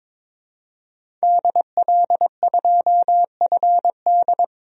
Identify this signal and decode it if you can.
Morse code, heard as a keyed tone, about 22 wpm, 700 Hz. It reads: DL2FD
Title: DXCC-100 Call Signs - 5 Characters - 22wpm